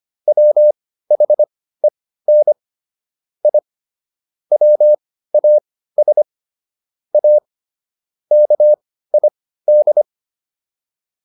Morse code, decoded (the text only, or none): WHEN I WAS A KID